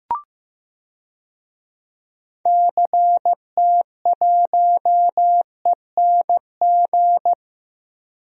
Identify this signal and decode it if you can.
Morse code, heard as a keyed tone, about 15 wpm, 700 Hz. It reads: CT1ENG
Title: DXCC-100 Call Signs - 6 Characters - Repeated 2x - 15wpm